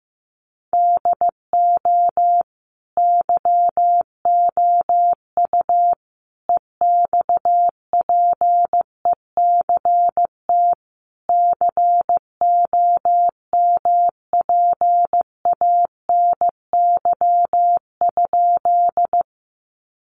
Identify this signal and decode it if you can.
Morse code, heard as a keyed tone, about 15 wpm, 700 Hz. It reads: DO YOU EXPECT COMPANY?
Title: Sentences from Top 1100 Words 15wpm